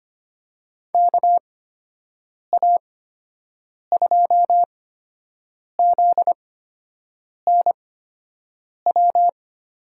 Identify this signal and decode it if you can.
Morse code, heard as a keyed tone, about 25 wpm, spaced as for 7 wpm, 700 Hz. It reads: KA2ZNW
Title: USA Call Signs - 6 Characters - 8x Character Spacing - 25wpm